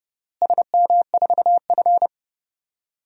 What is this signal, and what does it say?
Morse code, heard as a keyed tone, about 28 wpm, 700 Hz. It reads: SM4F
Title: DXCC-100 Call Signs - 4 Characters - Repeated 3x - 30wpm